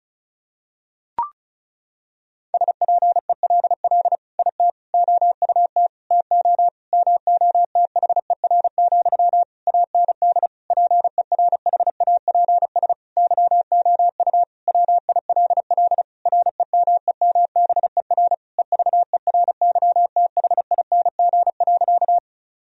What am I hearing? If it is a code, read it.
Morse code, heard as a keyed tone, about 34 wpm, 700 Hz. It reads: SPELL IT OUT TO MOTHER, AND PERHAPS YOU WILL REMEMBER EVERYTHING.
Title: Sentences from Top 1000 Words - Review Part 1 - 35wpm